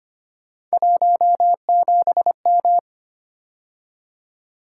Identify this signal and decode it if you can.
Morse code, heard as a keyed tone, about 25 wpm, 700 Hz. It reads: 17M
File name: Three Letter-Number Combinations - 25wpm